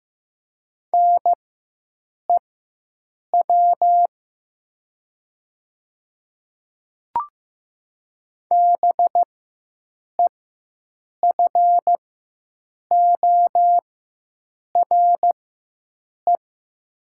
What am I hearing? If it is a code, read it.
Morse code, heard as a keyed tone, about 15 wpm, spaced as for 7 wpm, 700 Hz. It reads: NEW BEFORE
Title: 4x Character Spacing - Top 200 Words - 15wpm